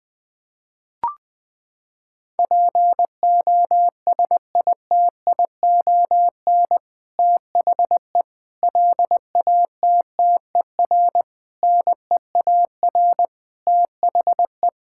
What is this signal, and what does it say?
Morse code, heard as a keyed tone, about 20 wpm, 700 Hz. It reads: POSITION THE LATTER NEAR THE
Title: Sentences from Top 1600 Words 20wpm